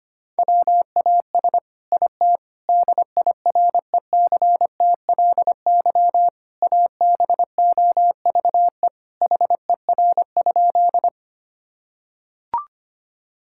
Morse code, heard as a keyed tone, about 25 wpm, 700 Hz. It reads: WAS IT DIRECTLY ABOVE HER?
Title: Sentences from Top 1000 Words - Review Part 5 - 25wpm